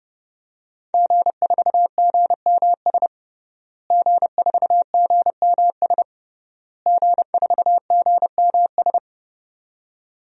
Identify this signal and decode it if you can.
Morse code, heard as a keyed tone, about 30 wpm, 700 Hz. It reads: G4GMS G4GMS G4GMS
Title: DXCC-100 Call Signs - 5 Characters - Repeated 3x - 30wpm